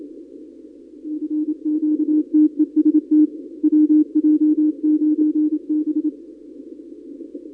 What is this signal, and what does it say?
Morse code, heard as a keyed tone, about 27 wpm, 300 Hz. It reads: FQTEST WJ9B
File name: contests / 2019-cw-wpx-kt5j-k5pi- -2330 0-31